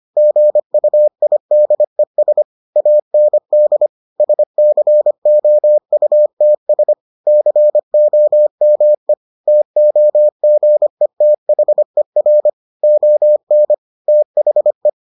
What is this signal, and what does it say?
Morse code, heard as a keyed tone, about 25 wpm, 600 Hz. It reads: GUIDES AND SCOUTS COME TOGETHER ON THE